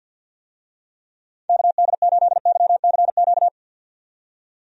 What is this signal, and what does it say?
Morse code, heard as a keyed tone, about 52 wpm, 700 Hz. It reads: KD8YCQ
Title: USA Call Signs - 6 Characters - 50wpm